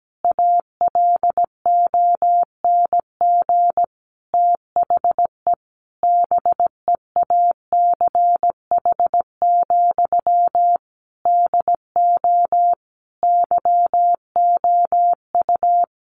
Morse code, heard as a keyed tone, about 17 wpm, 700 Hz. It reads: ALONG THE BEACH, DO YOU
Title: Sentences from Top 1500 Words 17wpm